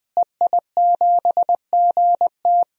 Morse code, heard as a keyed tone, about 20 wpm, 700 Hz. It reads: EI7GT
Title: DXCC-100 Call Signs - 5 Characters - Repeated 3x - 20wpm